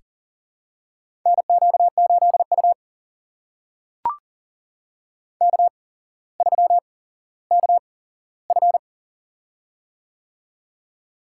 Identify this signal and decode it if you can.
Morse code, heard as a keyed tone, about 40 wpm, spaced as for 17 wpm, 700 Hz. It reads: NQ8U K3KF
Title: USA Call Signs - 4 Characters - 8x Character Spacing - 40wpm